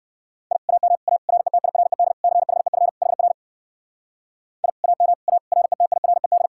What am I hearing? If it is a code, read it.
Morse code, heard as a keyed tone, about 67 wpm, 700 Hz. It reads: I AM A RETIRED CHEF HR I AM A RETIRED